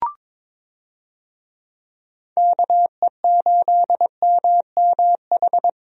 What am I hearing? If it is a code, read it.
Morse code, heard as a keyed tone, about 22 wpm, 700 Hz. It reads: KE8MMH
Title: USA Call Signs - 6 Characters - 22wpm